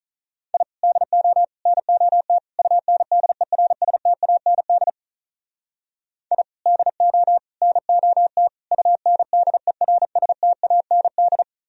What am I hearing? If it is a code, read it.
Morse code, heard as a keyed tone, about 44 wpm, 700 Hz. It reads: I DO NOT UNDERSTAND I DO NOT UNDERSTAND